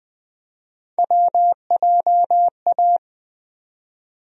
Morse code, heard as a keyed tone, about 20 wpm, 700 Hz. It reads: WJA
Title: USA Call Sign Postfixes - 3 Letters 20wpm